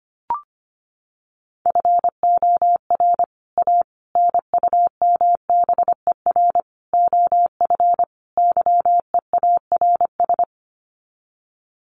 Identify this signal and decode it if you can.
Morse code, heard as a keyed tone, about 25 wpm, 700 Hz. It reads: FOR A NUMBER OF YEARS
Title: Sets of 5 Words Encore 25wpm